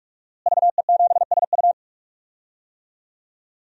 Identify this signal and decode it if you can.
Morse code, heard as a keyed tone, about 46 wpm, 700 Hz. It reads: VE7SU
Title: DXCC-100 Call Signs - 5 Characters - Repeated 2x - 45wpm